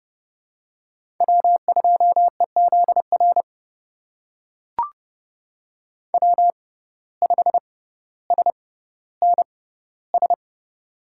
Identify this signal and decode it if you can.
Morse code, heard as a keyed tone, about 30 wpm, spaced as for 15 wpm, 700 Hz. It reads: W2EZR W5SNS
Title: USA Call Signs - 5 Characters - 6x Character Spacing - 30wpm